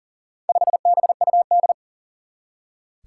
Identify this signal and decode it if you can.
Morse code, heard as a keyed tone, about 40 wpm, 700 Hz. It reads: 5BUD
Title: Four Challenging Letter-Number Combinations 40wpm